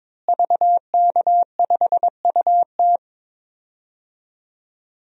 Morse code, heard as a keyed tone, about 22 wpm, 700 Hz. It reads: VK5UT
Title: DXCC-100 Call Signs - 5 Characters - Repeated 2x - 22wpm